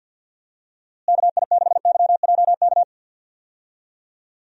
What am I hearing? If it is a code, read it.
Morse code, heard as a keyed tone, about 52 wpm, 700 Hz. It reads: KI6YJK
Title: USA Call Signs - 6 Characters - Repeated 3x - 50wpm